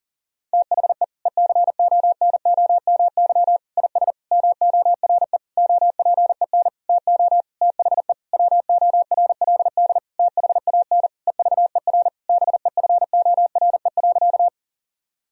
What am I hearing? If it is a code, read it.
Morse code, heard as a keyed tone, about 37 wpm, 700 Hz. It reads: THE ECONOMY IS MORE OPEN TO THE WORLD THAN EVER BEFORE.
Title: Sentences from Top 1000 Words - Review Part 3 - 40wpm